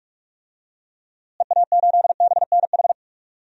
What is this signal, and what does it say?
Morse code, heard as a keyed tone, about 42 wpm, 700 Hz. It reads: EA8BNH
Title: DXCC-100 Call Signs - 6 Characters - Repeated 3x - 45wpm